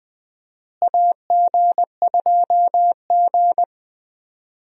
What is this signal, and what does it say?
Morse code, heard as a keyed tone, about 20 wpm, 700 Hz. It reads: AG2G
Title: USA Call Signs - 4 Characters - Repeated 3x - 20wpm